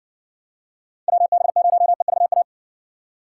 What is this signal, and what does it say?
Morse code, heard as a keyed tone, about 61 wpm, 700 Hz. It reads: WB0EVA